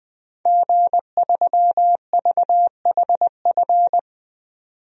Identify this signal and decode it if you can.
Morse code, heard as a keyed tone, about 20 wpm, 700 Hz. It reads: G3VHF
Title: DXCC-100 Call Signs - 5 Characters - 20wpm